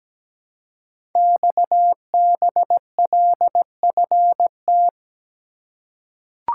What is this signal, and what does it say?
Morse code, heard as a keyed tone, about 17 wpm, 700 Hz. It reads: XBLFT
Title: Code Groups - 5 Letters - 17wpm